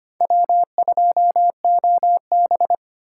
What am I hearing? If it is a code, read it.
Morse code, heard as a keyed tone, about 25 wpm, 700 Hz. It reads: W2OB
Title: USA Call Signs - 4 Characters - 2x Character Spacing - 25wpm